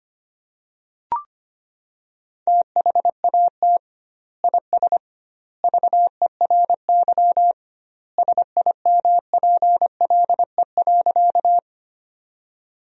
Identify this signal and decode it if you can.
Morse code, heard as a keyed tone, about 25 wpm, 700 Hz. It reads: THAT IS VERY SIMPLE.
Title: 2x Word-Spacing - Sentences from Top 400 Words - 25wpm